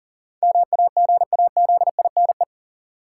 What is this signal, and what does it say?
Morse code, heard as a keyed tone, about 40 wpm, 700 Hz. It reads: MAGAZINE
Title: Sentences from Top 1000 Words - Review Part 7 - New Word Drill - 40wpm